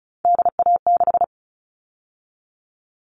Morse code, heard as a keyed tone, about 34 wpm, 700 Hz. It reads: DA6
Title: All Cut Numbers - 3-Digit 35wpm